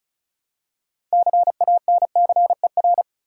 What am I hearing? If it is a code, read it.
Morse code, heard as a keyed tone, about 36 wpm, 700 Hz. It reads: CANCER
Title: Six Letter Words - 35wpm